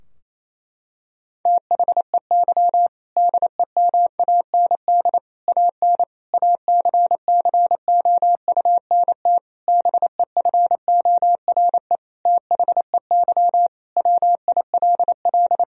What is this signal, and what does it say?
Morse code, heard as a keyed tone, about 28 wpm, 700 Hz. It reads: THEY DEMAND AN ACCOUNT BEFORE THEY WILL
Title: Sentences from Top 1000 Words - Review Part 4 - 28wpm